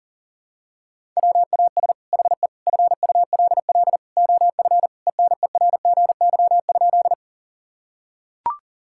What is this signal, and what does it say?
Morse code, heard as a keyed tone, about 40 wpm, 700 Hz. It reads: WAS HE FULL OF ENERGY?
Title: Sentences from Top 1000 Words - Review Part 6 - 40wpm